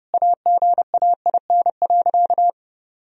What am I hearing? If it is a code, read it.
Morse code, heard as a keyed tone, about 30 wpm, 700 Hz. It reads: AGAIN.
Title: Sentences from Top 1000 Words - Review Part 3 - 30wpm